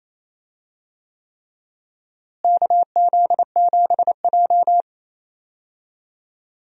Morse code, heard as a keyed tone, about 28 wpm, 700 Hz. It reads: KZ7J
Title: USA Call Signs - 4 Characters - Repeated 3x - 28wpm